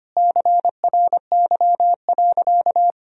Morse code, heard as a keyed tone, about 25 wpm, 700 Hz. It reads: CRY.
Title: Sentences from Top 500 Words and Abbreviations 25wpm